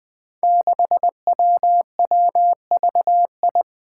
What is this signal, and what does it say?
Morse code, heard as a keyed tone, about 21 wpm, 700 Hz. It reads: 6WWVI
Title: Five Challenging Letter-Number Combinations 20wpm